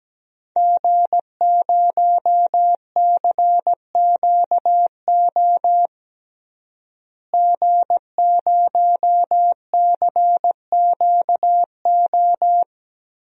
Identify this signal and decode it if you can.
Morse code, heard as a keyed tone, about 17 wpm, 700 Hz. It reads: G0CQO G0CQO
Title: DXCC-100 Call Signs - 5 Characters - Repeated 2x - 17wpm